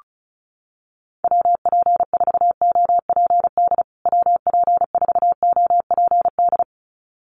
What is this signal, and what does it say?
Morse code, heard as a keyed tone, about 34 wpm, 700 Hz. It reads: WP4OPD WP4OPD
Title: USA Call Signs Repeated 2x 35wpm